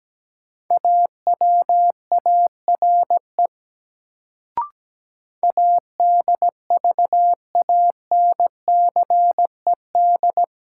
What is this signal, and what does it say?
Morse code, heard as a keyed tone, about 17 wpm, 700 Hz. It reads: AWARE ADVANCED
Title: Sentences from Top 1200 Words - New Word Drill Alt - 17wpm